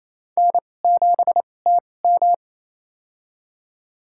Morse code, heard as a keyed tone, about 28 wpm, spaced as for 20 wpm, 700 Hz. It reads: N7TM
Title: USA Call Signs - 2x Character Spacing - 28wpm